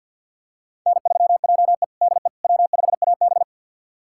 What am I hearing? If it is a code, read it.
Morse code, heard as a keyed tone, about 51 wpm, 700 Hz. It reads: N3JE DE W5AB
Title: USA Callsign DE Callsign - 4 Characters - 50wpm